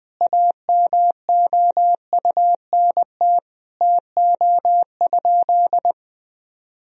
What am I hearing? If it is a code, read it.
Morse code, heard as a keyed tone, about 21 wpm, 700 Hz. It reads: AMOUNT TO?
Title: Sentences from Top 1000 Words - Review Part 4 - 20wpm